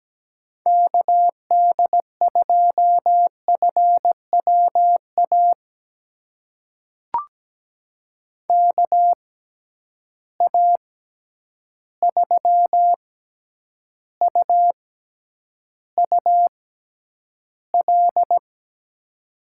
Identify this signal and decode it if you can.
Morse code, heard as a keyed tone, about 17 wpm, spaced as for 9 wpm, 700 Hz. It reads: KD2FWA KA3UUL
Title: USA Call Signs - 6 Characters - 6x Character Spacing - 17wpm